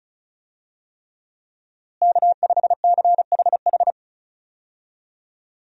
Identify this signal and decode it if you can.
Morse code, heard as a keyed tone, about 36 wpm, 700 Hz. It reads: K5CHH
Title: USA Call Signs - 5 Characters - Repeated 3x - 35wpm